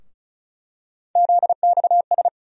Morse code, heard as a keyed tone, about 36 wpm, 700 Hz. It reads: ZXS
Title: USA Call Sign Postfixes - 3 Letters 35wpm